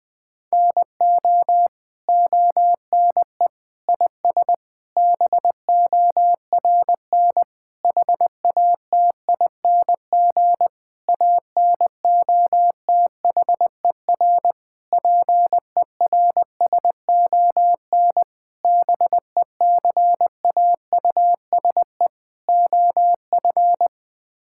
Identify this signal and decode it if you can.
Morse code, heard as a keyed tone, about 20 wpm, 700 Hz. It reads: NO ONE IS BORN HATING ANOTHER PERSON BECAUSE OF
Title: Sets of 9 Words - 20wpm